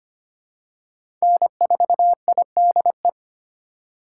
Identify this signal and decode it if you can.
Morse code, heard as a keyed tone, about 25 wpm, 700 Hz. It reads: N4IDE
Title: USA Call Signs - 3x Character Spacing - 25wpm